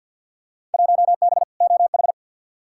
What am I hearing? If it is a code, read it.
Morse code, heard as a keyed tone, about 52 wpm, 700 Hz. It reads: 1B OH
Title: Field Day Exchanges - 50wpm